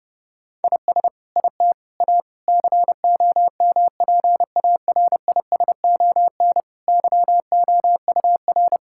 Morse code, heard as a keyed tone, about 30 wpm, 700 Hz. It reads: IS IT A COMPARISON YOUR
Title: Sentences from Top 1600 Words 30wpm